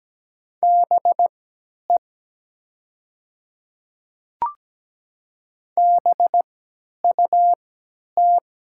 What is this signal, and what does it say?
Morse code, heard as a keyed tone, about 17 wpm, spaced as for 10 wpm, 690 Hz. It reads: BE BUT
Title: 3x Character Spacing - Top 100 Words - 17wpm